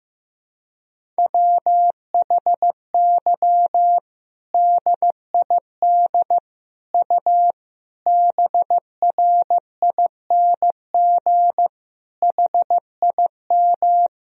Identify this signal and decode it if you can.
Morse code, heard as a keyed tone, about 15 wpm, 700 Hz. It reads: WHY DID U BRING HIM
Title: Sentences from Top 500 Words and Abbreviations 15wpm